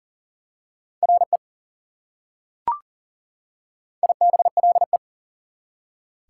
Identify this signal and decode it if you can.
Morse code, heard as a keyed tone, about 40 wpm, 700 Hz. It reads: RE IBLE
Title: Prefixes and Suffixes 40wpm